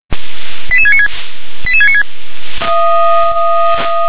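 Five-tone selective call (ZVEI), heard as quick runs of five tones, tones of about 70 ms. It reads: 88696 88696
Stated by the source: ZVEI five-tone selective call received on SDR.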